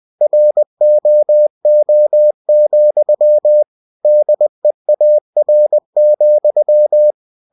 Morse code, heard as a keyed tone, about 20 wpm, 600 Hz. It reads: ROO, DEAR,
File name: cw-books / winnie-the-pooh / winnie-20wpm-page-0068